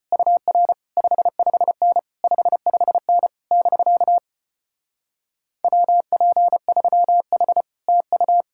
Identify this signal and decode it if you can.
Morse code, heard as a keyed tone, about 33 wpm, 700 Hz. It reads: UR 55N 55N <BK> WP3H TU